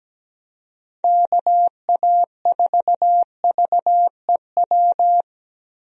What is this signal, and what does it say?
Morse code, heard as a keyed tone, about 17 wpm, 700 Hz. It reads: KA4VEW
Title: USA Call Signs - 6 Characters - 17wpm